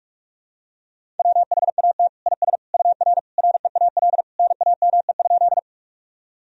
Morse code, heard as a keyed tone, about 46 wpm, 700 Hz. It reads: WHAT IS UR REAL NAME?
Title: Sentences from Top 300 Words and Abbreviations 45wpm